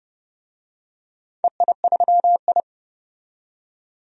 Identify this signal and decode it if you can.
Morse code, heard as a keyed tone, about 30 wpm, 700 Hz. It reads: EI3I
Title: DXCC-100 Call Signs - 4 Characters - 30wpm